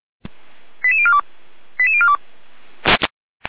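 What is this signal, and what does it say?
Five-tone selective call (ZVEI), heard as quick runs of five tones, tones of about 70 ms. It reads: 88042 88042
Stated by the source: Five-tone ZVEI selective call received on an SDR.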